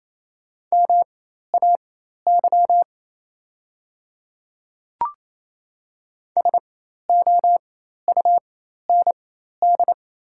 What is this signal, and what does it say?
Morse code, heard as a keyed tone, about 28 wpm, spaced as for 13 wpm, 700 Hz. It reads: MAY SOUND
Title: 4x Character Spacing - Top 100 Words - 28wpm